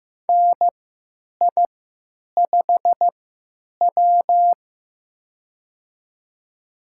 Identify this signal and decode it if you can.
Morse code, heard as a keyed tone, about 15 wpm, spaced as for 9 wpm, 700 Hz. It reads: NI5W
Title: USA Call Signs - 3x Character Spacing - 15wpm